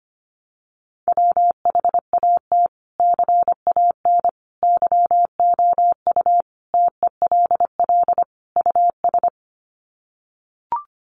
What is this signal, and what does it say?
Morse code, heard as a keyed tone, about 25 wpm, 700 Hz. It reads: WHAT CAN YOU TELL US
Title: Sets of 5 Words Encore 25wpm